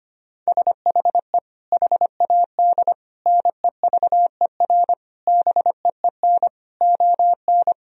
Morse code, heard as a keyed tone, about 25 wpm, 700 Hz. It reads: SHE HAD NEVER BEEN ON
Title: Sentences from Top 1000 Words - Review Part 2 - 25wpm